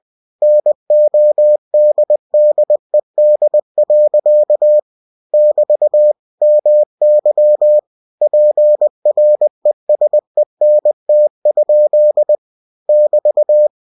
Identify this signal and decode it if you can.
Morse code, heard as a keyed tone, about 20 wpm, 600 Hz. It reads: NODDED. <BT> MY PRESENT? <BT>
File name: cw-books / winnie-the-pooh / winnie-20wpm-page-0056